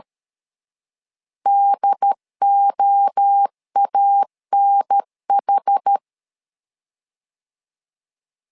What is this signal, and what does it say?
Morse code, heard as a keyed tone, about 13 wpm, 800 Hz. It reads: DOANH
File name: LEZIONI / 4 Q1S 12wpm 1802